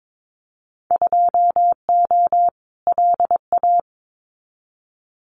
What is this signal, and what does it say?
Morse code, heard as a keyed tone, about 22 wpm, 700 Hz. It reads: 2O LA
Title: Winter Field Day Exchanges 22wpm